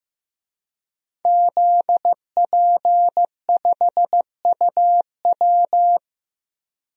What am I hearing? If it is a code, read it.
Morse code, heard as a keyed tone, about 15 wpm, 700 Hz. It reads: ZP5UW
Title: DXCC-100 Call Signs - 5 Characters - Repeated 2x - 15wpm